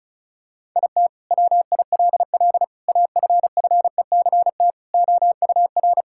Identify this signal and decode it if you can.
Morse code, heard as a keyed tone, about 36 wpm, 700 Hz. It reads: IT WILL AFFECT OUR
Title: Sentences from Top 1000 Words - Review Part 6 - 35wpm